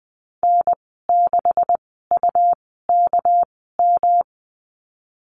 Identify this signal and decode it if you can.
Morse code, heard as a keyed tone, about 20 wpm, spaced as for 15 wpm, 700 Hz. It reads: N6UKM
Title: 200 Lesson 32g - Letter k - Call Signs 20wpm